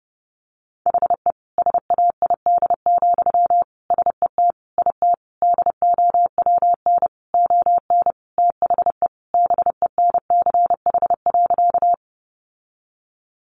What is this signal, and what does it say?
Morse code, heard as a keyed tone, about 30 wpm, 700 Hz. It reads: HE SAID, SET IT DOWN ON THE BENCH.
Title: Sentences from Top 1400 Words 30wpm